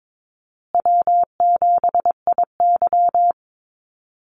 Morse code, heard as a keyed tone, about 22 wpm, 690 Hz. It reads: W7IY
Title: Easy Call Signs Repeated 2x - 22wpm